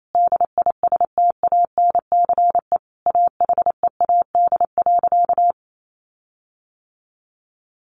Morse code, heard as a keyed tone, about 28 wpm, 700 Hz. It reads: DISTANCE AHEAD.
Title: Sentences from Top 1400 Words 28wpm